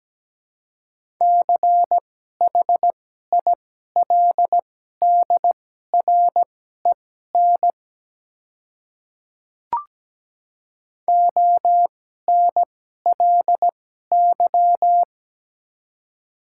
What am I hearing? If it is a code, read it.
Morse code, heard as a keyed tone, about 17 wpm, spaced as for 12 wpm, 690 Hz. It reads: CHILDREN ONLY
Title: 2x Character Spacing - Top 300 Words - 17wpm